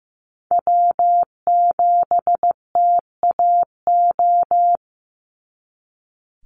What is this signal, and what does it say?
Morse code, heard as a keyed tone, about 15 wpm, 700 Hz. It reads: W7TAO
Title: NA SOTA Chaser Callsigns 15wpm